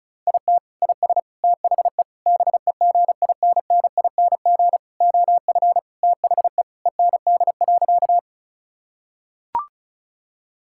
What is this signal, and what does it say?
Morse code, heard as a keyed tone, about 36 wpm, 700 Hz. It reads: IT IS THE BEGINNING OF THE END.
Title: Sentences from Top 1000 Words - Review Part 5 - 35wpm